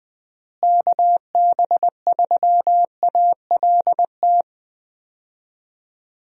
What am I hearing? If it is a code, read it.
Morse code, heard as a keyed tone, about 21 wpm, 700 Hz. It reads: KB3ALT
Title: USA Call Signs - 6 Characters - Repeated 3x - 20wpm